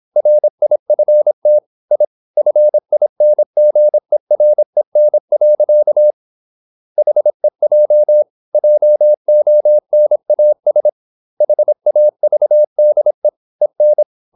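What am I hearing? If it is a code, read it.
Morse code, heard as a keyed tone, about 26 wpm, 600 Hz. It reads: RIFT I FINGEREN. HEJ JONAS HAVDE EN
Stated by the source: High-speed (QRQ) Morse at 26 wpm.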